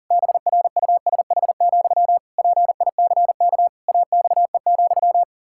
Morse code, heard as a keyed tone, about 40 wpm, 700 Hz. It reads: BRUSH, PICK AXE,
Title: Pangram Challenge v2 - 40wpm